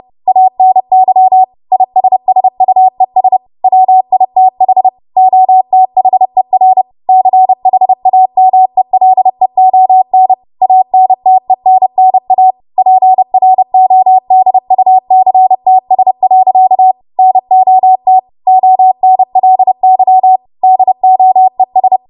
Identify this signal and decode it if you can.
Morse code, heard as a keyed tone, about 30 wpm, 750 Hz. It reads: ANY ISSUES WITH OTHER CHAMELEON ANTENNA PRODUCTS. NOT ONLY DOES
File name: Morse / 30WPM